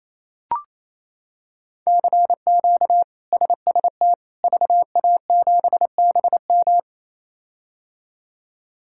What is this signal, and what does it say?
Morse code, heard as a keyed tone, about 28 wpm, 700 Hz. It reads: CQ SST VA7BM
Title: K1USN SST QSO Practice - 28wpm